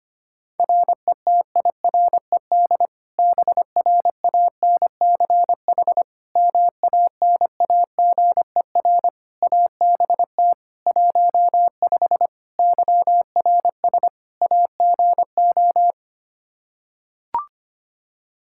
Morse code, heard as a keyed tone, about 25 wpm, 700 Hz. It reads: RETIRED BRANCH MANAGER ABT 15 YRS AGO
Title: Practice QSO - Exchange 3 - More Info - 25wpm